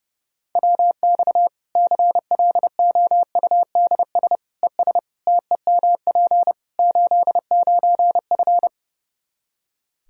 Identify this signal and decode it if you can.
Morse code, heard as a keyed tone, about 30 wpm, 700 Hz. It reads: WX CLOUDS ES TEMP 89F
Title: Practice QSO - Exchange 2 - Info - 30wpm